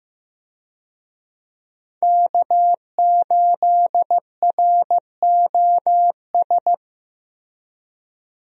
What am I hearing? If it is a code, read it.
Morse code, heard as a keyed tone, about 15 wpm, 700 Hz. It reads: K8ROS
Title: USA Call Signs - 5 Characters - Repeated 3x - 15wpm